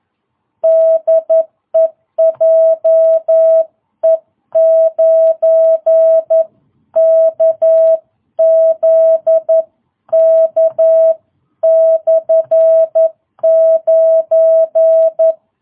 Morse code, heard as a keyed tone, about 11 wpm, 660 Hz. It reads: DE JE9KZK/9